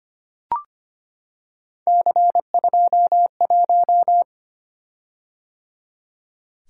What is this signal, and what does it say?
Morse code, heard as a keyed tone, about 25 wpm, 700 Hz. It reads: C21
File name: Three Letter-Number Combinations - 25wpm